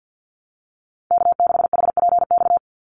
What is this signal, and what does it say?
Morse code, heard as a keyed tone, about 50 wpm, 700 Hz. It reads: K6HPX
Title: NA SOTA Chaser Callsigns 50wpm